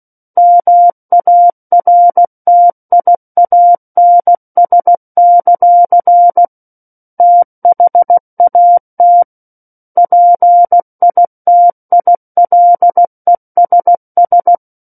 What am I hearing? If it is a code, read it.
Morse code, heard as a keyed tone, about 16 wpm, 700 Hz. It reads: MARTIANS; THAT PITILESS
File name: Book One / 07-Chapter 7 - 16 WPM